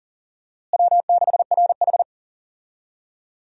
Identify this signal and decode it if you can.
Morse code, heard as a keyed tone, about 40 wpm, 700 Hz. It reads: W6RH
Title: USA Call Signs - 4 Characters - Repeated 3x - 40wpm